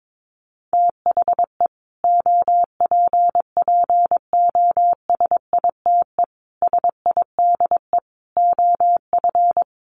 Morse code, heard as a keyed tone, about 22 wpm, 700 Hz. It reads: THE OPPOSITE SIDE OF
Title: Sets of 5 Words Encore 22wpm